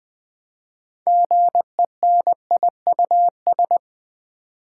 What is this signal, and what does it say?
Morse code, heard as a keyed tone, about 20 wpm, 700 Hz. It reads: GENIUS
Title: 6 Letter Words - First 3 Letters Given - 20wpm